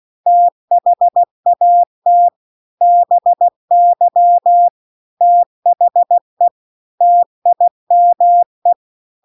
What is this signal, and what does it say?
Morse code, heard as a keyed tone, about 16 wpm, 700 Hz. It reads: THAT BY THE TIME
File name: 16WPM